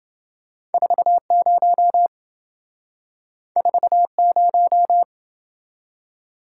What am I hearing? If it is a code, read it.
Morse code, heard as a keyed tone, about 27 wpm, 700 Hz. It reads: 40 40